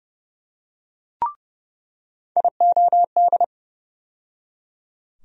Morse code, heard as a keyed tone, about 29 wpm, 700 Hz. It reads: IOD
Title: USA Call Sign Postfixes - 3 Letters 30wpm